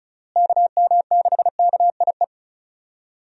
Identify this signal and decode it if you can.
Morse code, heard as a keyed tone, about 34 wpm, 700 Hz. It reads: KM6KIE
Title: USA Call Signs - 6 Characters - Repeated 3x - 35wpm